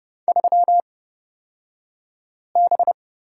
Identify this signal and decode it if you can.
Morse code, heard as a keyed tone, about 30 wpm, spaced as for 5 wpm, 700 Hz. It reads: 3B
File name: Lesson 36d - BT - Two Characters 30wpm